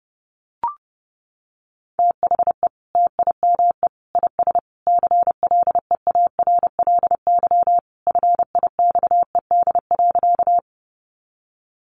Morse code, heard as a keyed tone, about 30 wpm, 700 Hz. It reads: THE TIME IS CLEARLY FIXED.
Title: Sentences from Top 1300 Words 30wpm